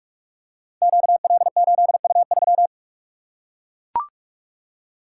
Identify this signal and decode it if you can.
Morse code, heard as a keyed tone, about 44 wpm, 700 Hz. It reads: QL8U3
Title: Code Groups - 5 Letter:Numbers - 45wpm